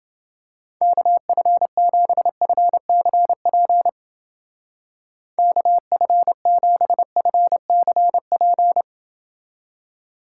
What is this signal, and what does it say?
Morse code, heard as a keyed tone, about 28 wpm, 700 Hz. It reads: KF7FCP KF7FCP